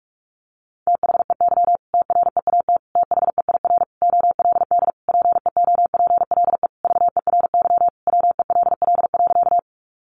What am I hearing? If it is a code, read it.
Morse code, heard as a keyed tone, about 44 wpm, 700 Hz. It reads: THEY TREAT THEIR OLD PEOPLE VERY WELL.
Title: Sentences from Top 1300 Words 45wpm